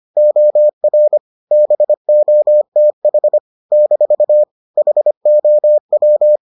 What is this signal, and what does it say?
Morse code, heard as a keyed tone, about 25 wpm, 600 Hz. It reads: OR BOTH - HOW